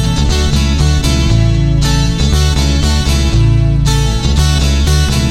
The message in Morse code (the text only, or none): none